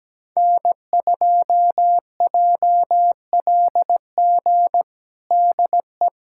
Morse code, heard as a keyed tone, about 17 wpm, 700 Hz. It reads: N2JLG DE
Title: USA Callsign DE Callsign 17wpm